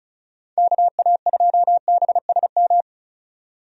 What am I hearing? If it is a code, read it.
Morse code, heard as a keyed tone, about 36 wpm, 700 Hz. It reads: KA2BSM
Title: USA Call Signs - 6 Characters - 35wpm